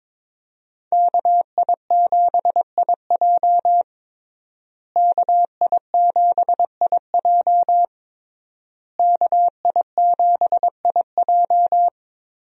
Morse code, heard as a keyed tone, about 22 wpm, 700 Hz. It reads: KI7IJ KI7IJ KI7IJ
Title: USA Call Signs - 5 Characters - Repeated 3x - 22wpm